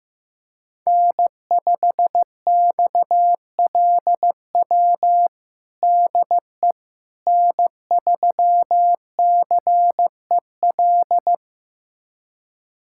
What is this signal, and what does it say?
Morse code, heard as a keyed tone, about 15 wpm, 700 Hz. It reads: N5XLW DE N3CEL
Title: USA Callsign DE Callsign 15wpm